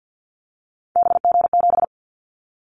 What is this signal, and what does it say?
Morse code, heard as a keyed tone, about 50 wpm, 700 Hz. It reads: BG7
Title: Callsign Prefixes for Top 100 DXCC - 50wpm